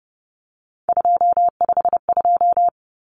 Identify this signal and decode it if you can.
Morse code, heard as a keyed tone, about 30 wpm, 700 Hz. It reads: 252
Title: Common Cut Numbers - 3-Digit 30wpm